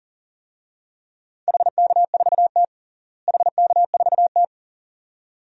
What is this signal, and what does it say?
Morse code, heard as a keyed tone, about 40 wpm, 700 Hz. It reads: HK4T HK4T
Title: DXCC-100 Call Signs - 4 Characters - Repeated 2x - 40wpm